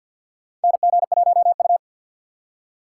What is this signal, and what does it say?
Morse code, heard as a keyed tone, about 47 wpm, 700 Hz. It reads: NG1U
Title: USA Call Signs - 4 Characters - 3x Character Spacing - 50wpm